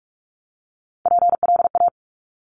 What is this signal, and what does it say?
Morse code, heard as a keyed tone, about 46 wpm, 700 Hz. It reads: PLA
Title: Top Three Letter Combinations in Words 45wpm